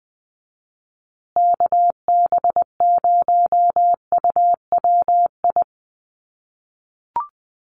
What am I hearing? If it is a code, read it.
Morse code, heard as a keyed tone, about 20 wpm, 700 Hz. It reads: KB0UWI
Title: USA Call Signs Repeated 2x 20wpm